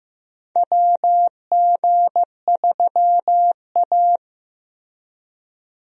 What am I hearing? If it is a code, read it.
Morse code, heard as a keyed tone, about 15 wpm, 700 Hz. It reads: WG3A
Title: USA Call Signs - 4 Characters - Repeated 3x - 15wpm